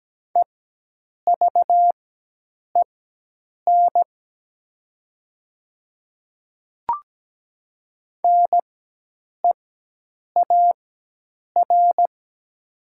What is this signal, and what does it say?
Morse code, heard as a keyed tone, about 17 wpm, spaced as for 8 wpm, 700 Hz. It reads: EVEN NEAR
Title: 4x Character Spacing - Top 300 Words - 17wpm